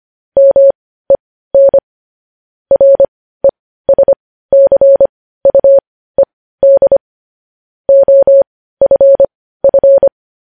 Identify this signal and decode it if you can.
Morse code, heard as a keyed tone, about 25 wpm, spaced as for 15 wpm, 550 Hz. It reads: MEN RESCUED OFF
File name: podcast / News headlines-1x-25x15